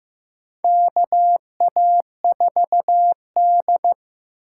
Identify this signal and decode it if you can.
Morse code, heard as a keyed tone, about 15 wpm, 700 Hz. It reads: KA4D
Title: USA Call Signs - 8x Character Spacing - 15wpm